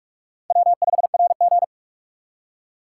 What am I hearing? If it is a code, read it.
Morse code, heard as a keyed tone, about 41 wpm, 700 Hz. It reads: W5RG
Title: USA Call Signs - 3x Character Spacing - 45wpm